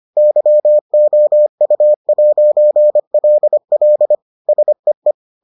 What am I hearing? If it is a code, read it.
Morse code, heard as a keyed tone, about 25 wpm, 600 Hz. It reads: YOU'LL SEE